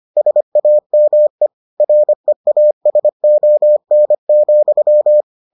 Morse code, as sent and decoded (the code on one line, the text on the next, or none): ... .- -- . / .-. .- ... --- -. --..--
SAME REASON,